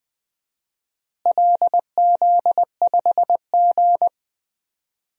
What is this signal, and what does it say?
Morse code, heard as a keyed tone, about 20 wpm, 700 Hz. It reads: LZ5G
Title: DXCC-100 Call Signs - 4 Characters - Repeated 3x - 20wpm